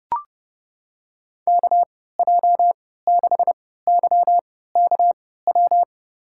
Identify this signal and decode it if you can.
Morse code, heard as a keyed tone, about 30 wpm, spaced as for 17 wpm, 700 Hz. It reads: KJ6YKW
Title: USA Call Signs - 3x Character Spacing - 30wpm